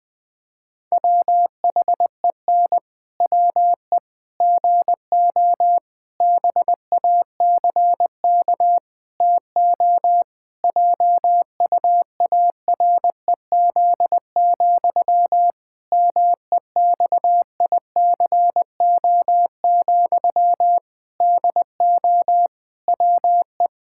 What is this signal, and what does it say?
Morse code, heard as a keyed tone, about 20 wpm, 700 Hz. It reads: WHEN WE GO BACK TO JUAREZ, MEXICO, DO WE
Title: Pangram Challenge v2 - 20wpm